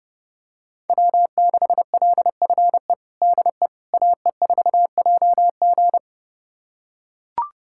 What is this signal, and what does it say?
Morse code, heard as a keyed tone, about 30 wpm, 700 Hz. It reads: W6LFE DE AE4JG
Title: USA Callsign DE Callsign - 5 Characters - 30wpm